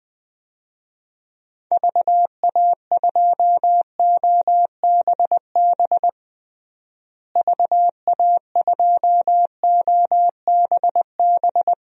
Morse code, heard as a keyed tone, about 20 wpm, 700 Hz. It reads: VA2OBB VA2OBB
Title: DXCC-100 Call Signs - 6 Characters - Repeated 3x - 20wpm